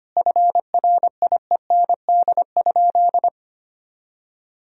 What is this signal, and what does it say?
Morse code, heard as a keyed tone, about 25 wpm, 700 Hz. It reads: FRIEND?
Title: Sentences from Top 300 Words and Abbreviations 25wpm